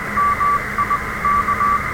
Morse code, heard as a keyed tone, about 20 wpm, 1160 Hz, with radio noise. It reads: MIK
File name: attachments / R7AO QRP 27.03.2016 Home made TRX UA1CEG